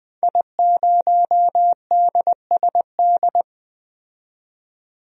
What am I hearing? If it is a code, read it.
Morse code, heard as a keyed tone, about 20 wpm, 700 Hz. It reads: I0DSD
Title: DXCC-100 Call Signs - 5 Characters - 20wpm